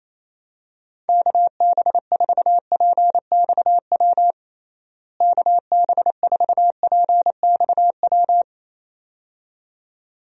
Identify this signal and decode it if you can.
Morse code, heard as a keyed tone, about 28 wpm, 700 Hz. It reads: KB4PXW KB4PXW
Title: USA Call Signs - 6 Characters - Repeated 2x - 28wpm